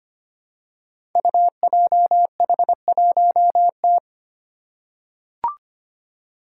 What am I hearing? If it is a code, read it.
Morse code, heard as a keyed tone, about 25 wpm, 700 Hz. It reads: UJH1T
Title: Five Challenging Letter-Number Combinations 25wpm